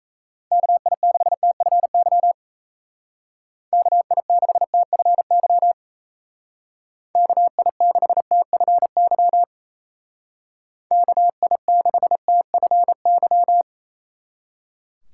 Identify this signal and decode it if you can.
Morse code, heard as a keyed tone, about 37 wpm, 700 Hz. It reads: KI6TFY KI6TFY KI6TFY KI6TFY